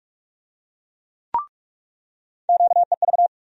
Morse code, heard as a keyed tone, about 47 wpm, 700 Hz. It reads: QEV
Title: USA Call Sign Postfixes - 3 Letters 45wpm